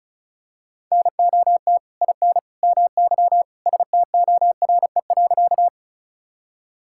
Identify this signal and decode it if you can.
Morse code, heard as a keyed tone, about 36 wpm, 700 Hz. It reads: NOT IN MY STORE.
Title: Sentences from Top 1000 Words - Review Part 5 - 35wpm